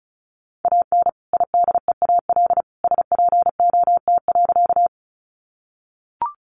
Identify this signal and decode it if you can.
Morse code, heard as a keyed tone, about 34 wpm, 700 Hz. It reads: AN IDEAL SPOT.
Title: Sentences from Top 1400 Words 35wpm